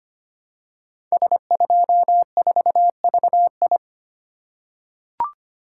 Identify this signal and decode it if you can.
Morse code, heard as a keyed tone, about 25 wpm, 700 Hz. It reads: S24VI
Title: Code Groups - 5 Letter:Numbers - 25wpm